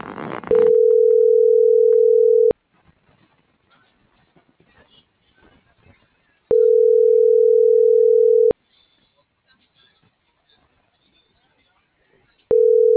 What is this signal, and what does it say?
Ringback tone (North America), 440 + 480 Hz.
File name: CENTRAL TERRESTRE / out claudiacruz - .65340